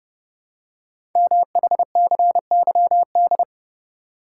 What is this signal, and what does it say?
Morse code, heard as a keyed tone, about 28 wpm, 700 Hz. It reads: MHCYD
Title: Code Groups - 5 Letters - 30wpm